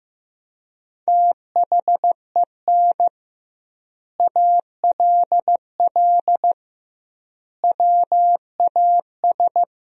Morse code, heard as a keyed tone, about 15 wpm, 700 Hz. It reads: THEN ALL WAS
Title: 2x Word-Spacing - Sentences from Top 500 Words - 15wpm